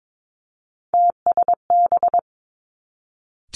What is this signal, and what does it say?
Morse code, heard as a keyed tone, about 22 wpm, 700 Hz. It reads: TSB
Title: Three Challenging Letter-Number Combinations 22wpm